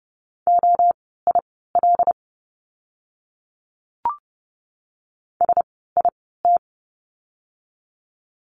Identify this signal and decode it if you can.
Morse code, heard as a keyed tone, about 30 wpm, spaced as for 17 wpm, 700 Hz. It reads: OIL SIT
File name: Lesson 17f - Letter u - Words 30wpm